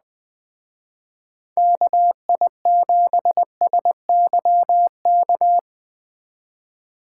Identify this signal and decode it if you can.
Morse code, heard as a keyed tone, about 20 wpm, 700 Hz. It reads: KI7SYK